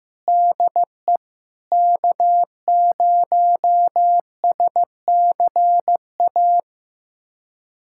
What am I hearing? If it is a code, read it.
Morse code, heard as a keyed tone, about 15 wpm, 700 Hz. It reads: DE K0SCA
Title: USA Callsign DE Callsign - 5 Characters - 15wpm